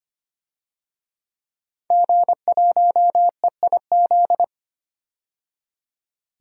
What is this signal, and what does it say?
Morse code, heard as a keyed tone, about 25 wpm, 700 Hz. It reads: G1EIZ
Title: DXCC-100 Call Signs - 5 Characters - 25wpm